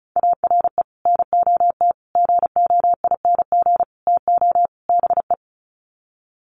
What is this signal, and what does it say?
Morse code, heard as a keyed tone, about 34 wpm, 700 Hz. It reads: ARE NOT GOING TO BE
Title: Sets of 5 Words Encore 35wpm